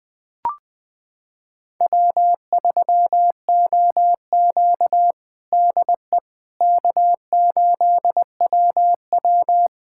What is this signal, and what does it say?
Morse code, heard as a keyed tone, about 21 wpm, 700 Hz. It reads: W3OQ DE K8WW
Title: USA Callsign DE Callsign - 4 Characters - 20wpm